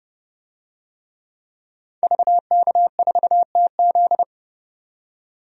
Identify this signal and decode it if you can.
Morse code, heard as a keyed tone, about 32 wpm, 700 Hz. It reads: VK4TZ
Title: DXCC-100 Call Signs - 5 Characters - 30wpm